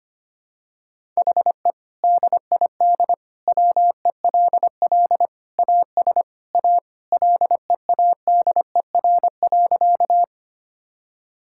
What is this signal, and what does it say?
Morse code, heard as a keyed tone, about 25 wpm, 700 Hz. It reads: HE DID WELL AS A LEADER.
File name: practice / 25wpm / Sentences with 2-Word Phrases - 25wpm